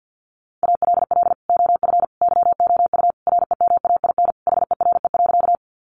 Morse code, heard as a keyed tone, about 50 wpm, 700 Hz. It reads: ALL OF YOU REMAIN HERE.
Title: Sentences from Top 1200 Words 50wpm